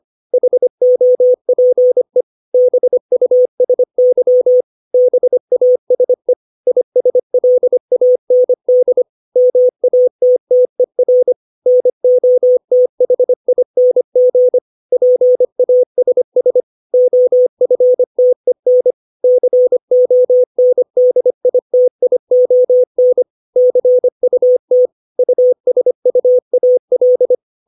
Morse code, heard as a keyed tone, about 25 wpm, 500 Hz. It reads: HOPE BUSY BASE ISLAND MATTER NOTHING PASS OFTEN CONDITION CUT USUAL
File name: 632 words / 632 Words 25WPM